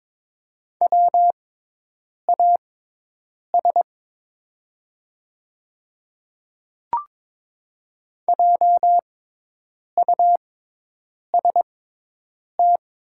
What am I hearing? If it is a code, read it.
Morse code, heard as a keyed tone, about 22 wpm, spaced as for 8 wpm, 690 Hz. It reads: WAS JUST
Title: 6x Character Spacing - Top 300 Words - 22wpm